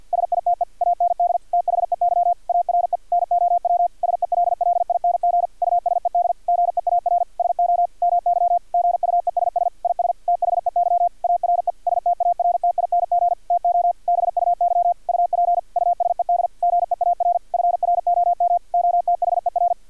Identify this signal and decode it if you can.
Morse code, heard as a keyed tone, about 50 wpm, 700 Hz. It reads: SITE AND THEY ARE NOW SELLING USED GEAR SO MY GUESS IS THEY ARE STARTING TO BUY UP USED GEAR FROM OTHER
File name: archived / CRSnet